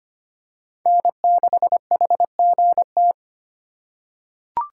Morse code, heard as a keyed tone, about 25 wpm, 700 Hz. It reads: N6HGT
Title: USA Call Signs - 4x Character Spacing - 25wpm